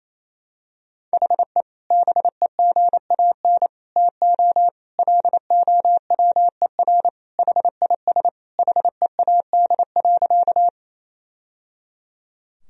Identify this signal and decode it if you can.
Morse code, heard as a keyed tone, about 28 wpm, 700 Hz. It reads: HE BEGAN TO LOWER HIS HEAD.
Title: Sentences from Top 1000 Words - Review Part 4 - 28wpm